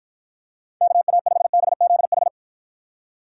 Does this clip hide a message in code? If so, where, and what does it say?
Yes: Morse code from 0.8 to 2.3 s, heard as a keyed tone, about 49 wpm, 700 Hz. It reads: KA5BZH